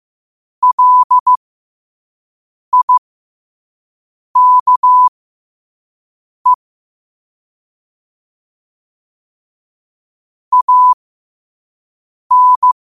Morse code, heard as a keyed tone, about 15 wpm, spaced as for 5 wpm, 1000 Hz. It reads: LIKE AN
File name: qotdfiles / 5wpm